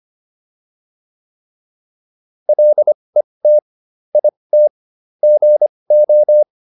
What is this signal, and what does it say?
Morse code, heard as a keyed tone, about 25 wpm, spaced as for 20 wpm, 600 Hz. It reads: LET IT GO